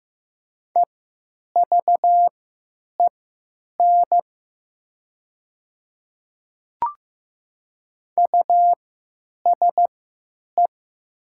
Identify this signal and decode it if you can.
Morse code, heard as a keyed tone, about 15 wpm, spaced as for 8 wpm, 700 Hz. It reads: EVEN USE
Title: 3x Character Spacing - Top 500 Words - 15wpm